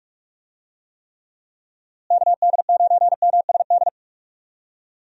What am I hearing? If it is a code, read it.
Morse code, heard as a keyed tone, about 43 wpm, 700 Hz. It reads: KD9MSD
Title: USA Call Signs - 6 Characters - Repeated 2x - 45wpm